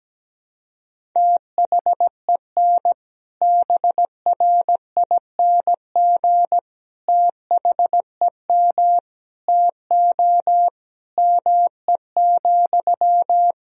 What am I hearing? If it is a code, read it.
Morse code, heard as a keyed tone, about 17 wpm, 700 Hz. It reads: THEN BRING THEM TO ME,
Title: Sentences from Top 500 Words and Abbreviations 17wpm